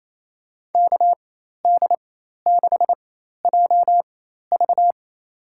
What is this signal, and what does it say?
Morse code, heard as a keyed tone, about 28 wpm, spaced as for 13 wpm, 700 Hz. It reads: KD6JV
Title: USA Call Signs - 4x Character Spacing - 28wpm